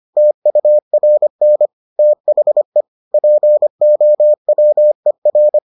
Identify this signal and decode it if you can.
Morse code, heard as a keyed tone, about 25 wpm, 600 Hz. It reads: TURN THE POWER